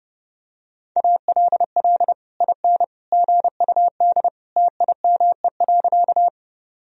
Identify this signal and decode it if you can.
Morse code, heard as a keyed tone, about 30 wpm, 700 Hz. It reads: ALL IN GUD TIME.
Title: Sentences from Top 200 Words and Abbreviations 30wpm